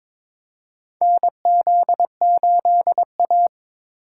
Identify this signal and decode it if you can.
Morse code, heard as a keyed tone, about 22 wpm, 700 Hz. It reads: NZ8A
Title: USA Call Signs - 4 Characters - 2x Character Spacing - 22wpm